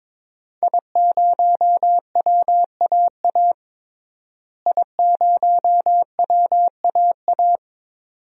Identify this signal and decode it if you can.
Morse code, heard as a keyed tone, about 21 wpm, 700 Hz. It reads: I0WAA I0WAA
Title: DXCC-100 Call Signs - 5 Characters - Repeated 2x - 22wpm